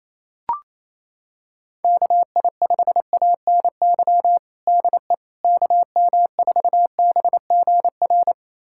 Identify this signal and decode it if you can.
Morse code, heard as a keyed tone, about 28 wpm, 700 Hz. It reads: KI5ANY DE KM4BGR
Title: USA Callsign DE Callsign 28wpm